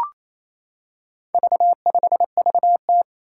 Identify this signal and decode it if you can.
Morse code, heard as a keyed tone, about 27 wpm, 700 Hz. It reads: V5VT
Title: Four Challenging Letter-Number Combinations 28wpm